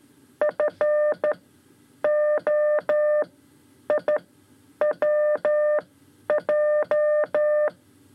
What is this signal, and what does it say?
Morse code, heard as a keyed tone, about 13 wpm, spaced as for 9 wpm, 600 Hz. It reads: FOIWJ
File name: 5 Lezione CW 2